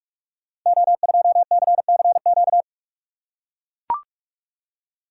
Morse code, heard as a keyed tone, about 44 wpm, 700 Hz. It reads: O2CCQ
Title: Code Groups - 5 Letter:Numbers - 45wpm